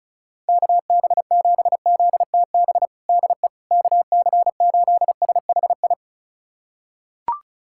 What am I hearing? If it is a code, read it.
Morse code, heard as a keyed tone, about 36 wpm, 700 Hz. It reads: KB7ZTB DE KC8SHI
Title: USA Callsign DE Callsign - 6 Characters - 35wpm